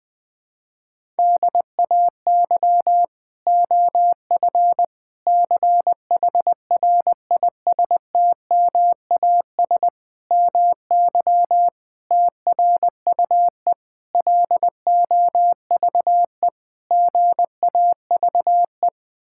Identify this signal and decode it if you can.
Morse code, heard as a keyed tone, about 21 wpm, 700 Hz. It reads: DAY OF CHRISTMAS MY TRUE LOVE GAVE
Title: Sentences with 9-Word Phrases - Spoken Beginning - 20wpm